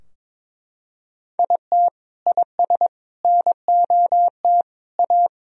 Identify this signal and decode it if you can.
Morse code, heard as a keyed tone, about 22 wpm, 700 Hz. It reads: IT IS NOT A
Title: Sentences from Top 1000 Words - Review Part 2 - 22wpm